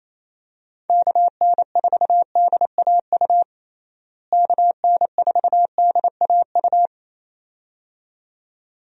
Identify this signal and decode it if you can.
Morse code, heard as a keyed tone, about 28 wpm, 700 Hz. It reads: KN4DAU KN4DAU
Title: USA Call Signs - 6 Characters - Repeated 2x - 28wpm